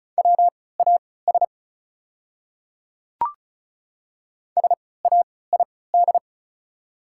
Morse code, heard as a keyed tone, about 35 wpm, spaced as for 20 wpm, 690 Hz. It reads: WAS SAID
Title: 3x Character Spacing - Top 500 Words - 35wpm